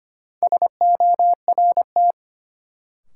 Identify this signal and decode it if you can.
Morse code, heard as a keyed tone, about 25 wpm, 700 Hz. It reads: SORT
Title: Sentences from Top 1300 Words - New Word Drill Alt - 25wpm